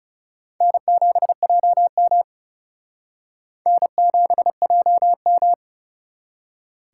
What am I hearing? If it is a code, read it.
Morse code, heard as a keyed tone, about 40 wpm, 700 Hz. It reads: N7JM N7JM